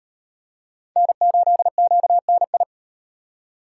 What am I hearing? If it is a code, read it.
Morse code, heard as a keyed tone, about 38 wpm, 700 Hz. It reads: N8QNI